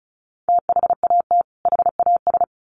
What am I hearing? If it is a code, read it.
Morse code, heard as a keyed tone, about 34 wpm, 700 Hz. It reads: THAT HAS
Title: Sets of 2 Words 35wpm